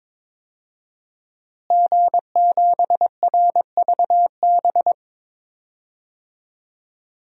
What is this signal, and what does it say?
Morse code, heard as a keyed tone, about 22 wpm, 700 Hz. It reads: G7RVB
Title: DXCC-100 Call Signs - 5 Characters - Repeated 2x - 22wpm